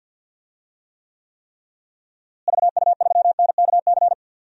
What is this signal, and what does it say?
Morse code, heard as a keyed tone, about 48 wpm, 700 Hz. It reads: VU3NKC
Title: DXCC-100 Call Signs - 6 Characters - Repeated 3x - 50wpm